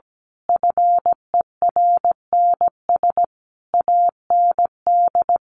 Morse code, heard as a keyed tone, about 17 wpm, 700 Hz. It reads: FERNS AND
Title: A Sound Of Thunder 17wpm